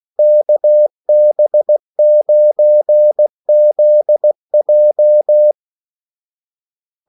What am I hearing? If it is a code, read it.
Morse code, heard as a keyed tone, about 16 wpm, 600 Hz. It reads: KB9ZJ
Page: Kaldesignaler 16 wpm